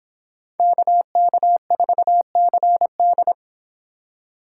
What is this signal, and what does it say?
Morse code, heard as a keyed tone, about 26 wpm, 700 Hz. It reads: KK4CD